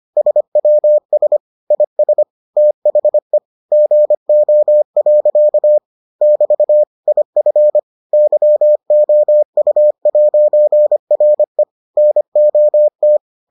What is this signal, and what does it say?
Morse code, heard as a keyed tone, about 25 wpm, 600 Hz. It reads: SWS IS THE GO. <BT> IF YOU'RE NOT